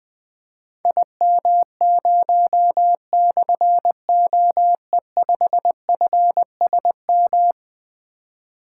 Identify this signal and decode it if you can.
Morse code, heard as a keyed tone, about 20 wpm, 700 Hz. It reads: IM0/OE5FSM
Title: Difficult Call Signs 20wpm